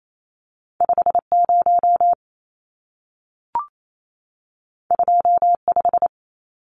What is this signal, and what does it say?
Morse code, heard as a keyed tone, about 28 wpm, 700 Hz. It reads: 50 25
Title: Numbers - 2-Digit 28wpm